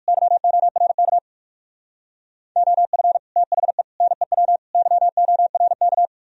Morse code, heard as a keyed tone, about 57 wpm, 700 Hz. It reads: YORK OF THE NEW YORK